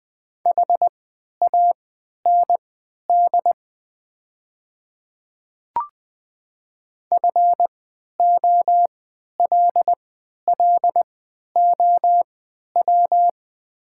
Morse code, heard as a keyed tone, about 20 wpm, spaced as for 11 wpm, 700 Hz. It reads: HAND FOLLOW
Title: 3x Character Spacing - Top 200 Words - 20wpm